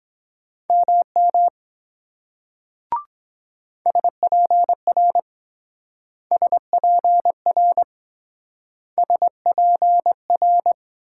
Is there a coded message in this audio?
Yes: Morse code from 0.7 to 10.7 s, heard as a keyed tone, about 23 wpm, 700 Hz. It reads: MM SPR SPR SPR